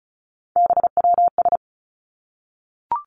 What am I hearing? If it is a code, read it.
Morse code, heard as a keyed tone, about 36 wpm, 700 Hz. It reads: BWS
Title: Three Challenging Letter-Number Combinations 35wpm